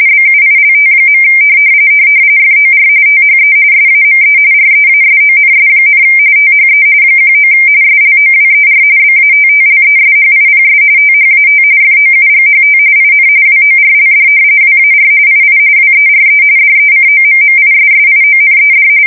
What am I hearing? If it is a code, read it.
RTTY, about 45.45 baud. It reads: BUILT. FOR THE SAME REASONS, THE 19 SLOWER BATTLESHIPS THAT FOUGHT IN WORLD WAR II WERE DECOMMISSIONED
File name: fsk170 / battleshipnj